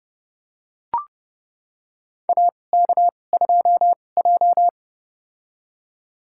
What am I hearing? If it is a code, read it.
Morse code, heard as a keyed tone, about 30 wpm, spaced as for 22 wpm, 700 Hz. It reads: AK2J
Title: USA Call Signs - 2x Character Spacing - 30wpm